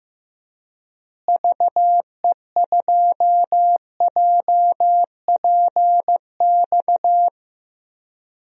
Morse code, heard as a keyed tone, about 15 wpm, 700 Hz. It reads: VE2JPX
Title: DXCC-100 Call Signs - 6 Characters - Repeated 2x - 15wpm